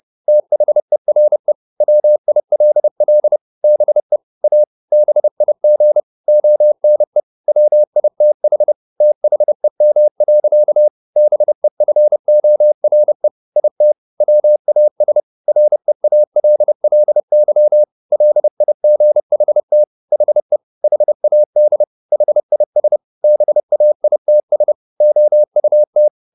Morse code, heard as a keyed tone, about 30 wpm, 600 Hz. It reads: THERE WILL BE A BIG ONE WITH THEM. BEFORE IT WAS REALLY LIGHT HE HAD HIS BAITS OUT
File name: the old man and the sea / the old man and the sea 1 30wpm